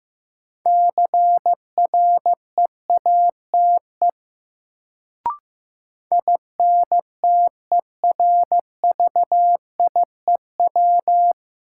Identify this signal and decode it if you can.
Morse code, heard as a keyed tone, about 15 wpm, 700 Hz. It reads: CREATE INTERVIEW
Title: Sentences from Top 1600 Words - New Word Drill Alt - 15wpm